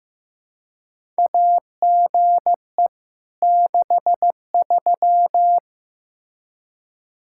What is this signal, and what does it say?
Morse code, heard as a keyed tone, about 15 wpm, 700 Hz. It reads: AGE 63
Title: K1USN SST QSO Practice 15wpm